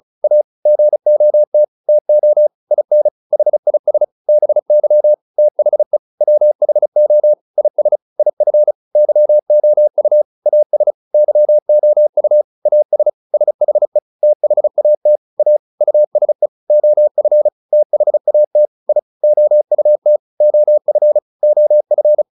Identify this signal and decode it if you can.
Morse code, heard as a keyed tone, about 34 wpm, 600 Hz. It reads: A GOT TO IN HIS BY THE WHO IS IF YOU AS YOU AS SHE THAT A USE OF THAT I OUT OF OF